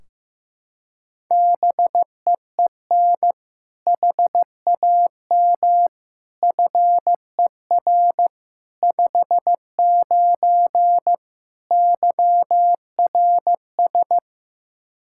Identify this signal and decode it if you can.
Morse code, heard as a keyed tone, about 15 wpm, 700 Hz. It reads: BEEN HAM FER 59 YRS
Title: Practice QSO - Exchange 3 - More Info - 15wpm